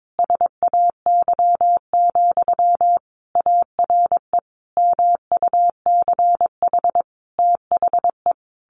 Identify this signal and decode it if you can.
Morse code, heard as a keyed tone, about 22 wpm, 700 Hz. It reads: SAY, ARE MUCH THE
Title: Sentences from Top 1100 Words 22wpm